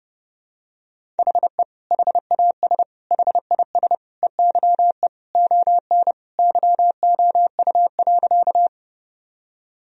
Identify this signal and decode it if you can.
Morse code, heard as a keyed tone, about 30 wpm, 700 Hz. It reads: HE HAS HIS EYE ON YOU.
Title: Sentences from Top 1000 Words - Review Part 1 - 30wpm